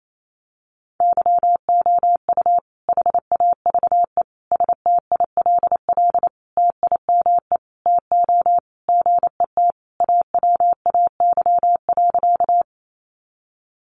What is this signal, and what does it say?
Morse code, heard as a keyed tone, about 28 wpm, 700 Hz. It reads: YOU HAVE STILL TIME TO GET AWAY.
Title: Sentences from Top 600 Words 28wpm